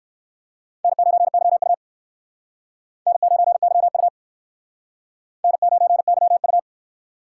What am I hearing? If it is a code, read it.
Morse code, heard as a keyed tone, about 57 wpm, 700 Hz. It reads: N9YU N9YU N9YU